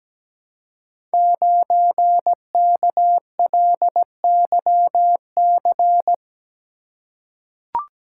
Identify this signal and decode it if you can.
Morse code, heard as a keyed tone, about 17 wpm, 700 Hz. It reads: 9KLYC
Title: Code Groups - 5 Letter:Numbers - 17wpm